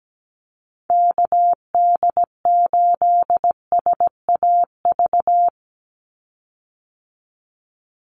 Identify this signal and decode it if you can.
Morse code, heard as a keyed tone, about 17 wpm, 700 Hz. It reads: KD8SAV
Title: Call Signs Repeated 2x - 17wpm